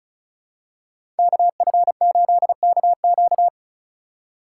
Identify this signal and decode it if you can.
Morse code, heard as a keyed tone, about 36 wpm, 700 Hz. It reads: KF8KQ
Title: USA Call Signs - 5 Characters - 8x Character Spacing - 35wpm